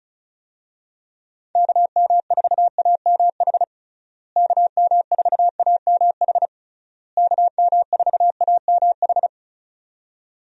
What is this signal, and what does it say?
Morse code, heard as a keyed tone, about 34 wpm, 700 Hz. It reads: KM4AMH KM4AMH KM4AMH
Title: USA Call Signs - 6 Characters - Repeated 3x - 35wpm